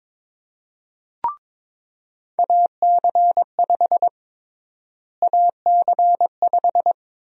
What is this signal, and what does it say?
Morse code, heard as a keyed tone, about 22 wpm, 700 Hz. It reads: AC5 AC5
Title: USA Call Sign Prefixes - Repeated 2x - 22wpm